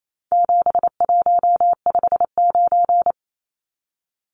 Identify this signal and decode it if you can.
Morse code, heard as a keyed tone, about 28 wpm, 700 Hz. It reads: 7159
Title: Numbers - 4-Digit 28wpm